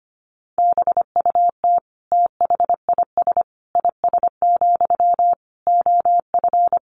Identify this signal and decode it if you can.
Morse code, heard as a keyed tone, about 25 wpm, 700 Hz. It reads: BUT THIS IS, OF
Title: Sentences from Top 1500 Words 25wpm